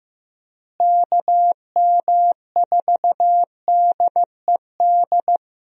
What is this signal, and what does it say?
Morse code, heard as a keyed tone, about 15 wpm, 700 Hz. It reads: KM4DED
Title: USA Call Signs - 6 Characters - Repeated 3x - 15wpm